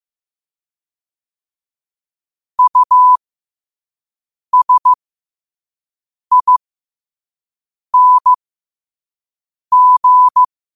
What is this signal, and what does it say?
Morse code, heard as a keyed tone, about 15 wpm, spaced as for 5 wpm, 1000 Hz. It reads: USING
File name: qotdfiles / 5wpm